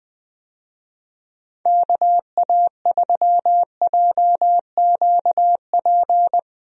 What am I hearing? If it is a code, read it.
Morse code, heard as a keyed tone, about 20 wpm, 700 Hz. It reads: KA3JQP
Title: USA Call Signs - 6 Characters - Repeated 3x - 20wpm